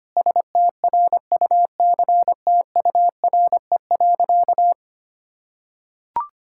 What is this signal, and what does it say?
Morse code, heard as a keyed tone, about 25 wpm, 700 Hz. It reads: STRUCTURE.
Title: Sentences from Top 1000 Words - Review Part 6 - 25wpm